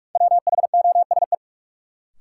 Morse code, heard as a keyed tone, about 41 wpm, 700 Hz. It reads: WHOSE
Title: Sentences from Top 1200 Words - New Word Drill Alt - 45wpm